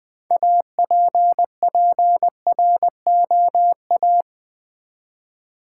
Morse code, heard as a keyed tone, about 20 wpm, 700 Hz. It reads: APPROA
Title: Build A Word - 2 Letters At A Time - 8-Letter Words - 20wpm